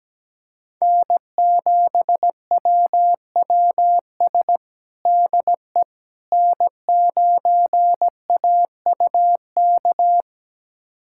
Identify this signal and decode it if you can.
Morse code, heard as a keyed tone, about 17 wpm, 700 Hz. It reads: N7WWS DE N9AUK
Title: USA Callsign DE Callsign - 5 Characters - 17wpm